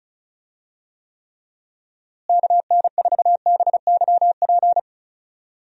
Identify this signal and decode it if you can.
Morse code, heard as a keyed tone, about 34 wpm, 700 Hz. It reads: KN4BYP
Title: USA Call Signs - 6 Characters - Repeated 2x - 35wpm